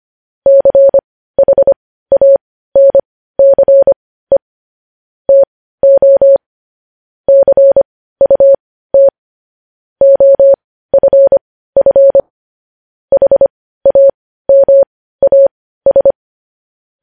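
Morse code, heard as a keyed tone, about 25 wpm, spaced as for 15 wpm, 550 Hz. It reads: CHANCE TO CUT OFF HAMAS
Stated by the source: Morse code news headlines.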